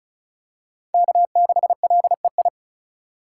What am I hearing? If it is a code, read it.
Morse code, heard as a keyed tone, about 34 wpm, 700 Hz. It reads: K6LEI
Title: USA Call Signs - 5 Characters - 3x Character Spacing - 35wpm